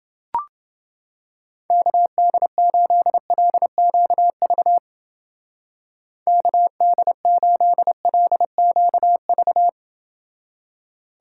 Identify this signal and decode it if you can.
Morse code, heard as a keyed tone, about 30 wpm, 700 Hz. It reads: KD8LQV KD8LQV